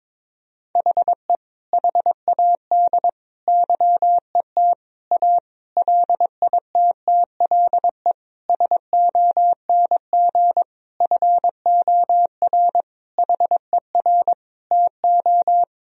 Morse code, heard as a keyed tone, about 21 wpm, 700 Hz. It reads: HE HAD YET A LITTLE SONG FOR HER TO
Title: Sentences from Top 1000 Words - Review Part 2 - 22wpm